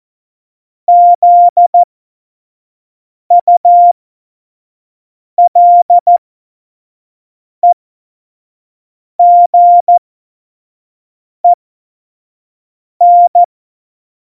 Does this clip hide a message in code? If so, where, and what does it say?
Yes: Morse code from 0.9 to 13.4 s, heard as a keyed tone, about 14 wpm, spaced as for 5 wpm, 700 Hz. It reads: ZULEGEN